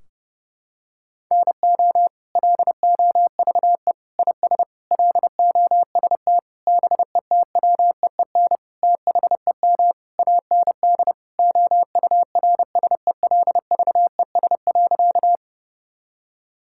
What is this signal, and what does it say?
Morse code, heard as a keyed tone, about 30 wpm, 700 Hz. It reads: NO LOVE IS LOST BETWEEN THEM AND OURSELVES.
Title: Sentences from Top 1600 Words 30wpm